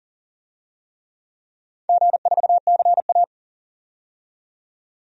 Morse code, heard as a keyed tone, about 39 wpm, 700 Hz. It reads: G4CA
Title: DXCC-100 Call Signs - 4 Characters - 40wpm